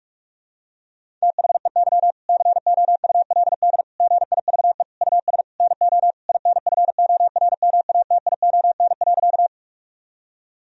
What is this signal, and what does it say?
Morse code, heard as a keyed tone, about 46 wpm, 700 Hz. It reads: THEY COULD GIVE US NO INFORMATION.
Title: Sentences from Top 1000 Words - Review Part 3 - 45wpm